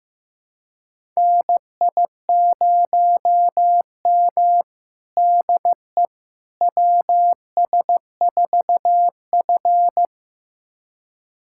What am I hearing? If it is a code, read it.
Morse code, heard as a keyed tone, about 16 wpm, 700 Hz. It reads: NI0M DE WS4F
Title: USA Callsign DE Callsign - 4 Characters - 15wpm